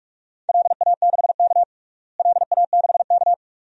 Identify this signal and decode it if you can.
Morse code, heard as a keyed tone, about 41 wpm, 700 Hz. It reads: LA6K LA6K
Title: DXCC-100 Call Signs - 4 Characters - Repeated 2x - 45wpm